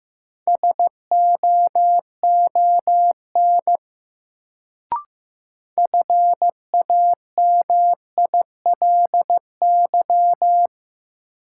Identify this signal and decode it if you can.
Morse code, heard as a keyed tone, about 15 wpm, 700 Hz. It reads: SOON FAMILY
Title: Sentences from Top 400 Words - New Word Drill - 15wpm